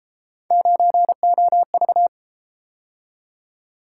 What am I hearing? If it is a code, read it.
Morse code, heard as a keyed tone, about 32 wpm, 700 Hz. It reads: 9OV